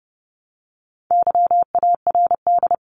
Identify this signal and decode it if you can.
Morse code, heard as a keyed tone, about 30 wpm, 700 Hz. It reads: YARD
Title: Top 1751-2000 Words 30wpm